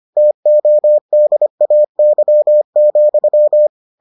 Morse code, heard as a keyed tone, about 25 wpm, 600 Hz. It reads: TODAY,